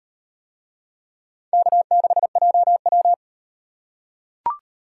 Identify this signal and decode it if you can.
Morse code, heard as a keyed tone, about 38 wpm, 700 Hz. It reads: K6JW